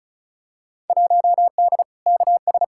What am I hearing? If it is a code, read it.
Morse code, heard as a keyed tone, about 36 wpm, 700 Hz. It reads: 1D KS
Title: Field Day Exchanges - 35wpm